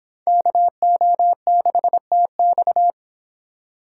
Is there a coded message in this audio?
Yes: Morse code from 0.3 to 2.9 s, heard as a keyed tone, about 26 wpm, 700 Hz. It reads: KO6TX